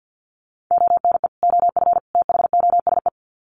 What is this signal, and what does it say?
Morse code, heard as a keyed tone, about 48 wpm, 700 Hz. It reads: ONE OF THOSE
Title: Sets of 3 Words 50wpm